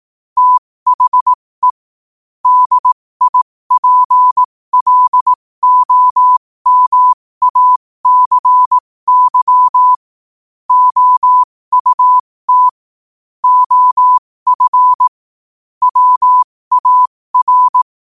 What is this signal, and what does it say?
Morse code, heard as a keyed tone, about 18 wpm, spaced as for 15 wpm, 1000 Hz. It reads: THE DIPLOMACY OUT OF WAR
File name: qotdfiles / 15wpm